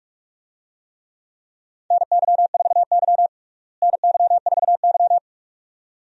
Morse code, heard as a keyed tone, about 46 wpm, 700 Hz. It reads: NY4Y NY4Y
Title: USA Call Signs - 4 Characters - Repeated 2x - 45wpm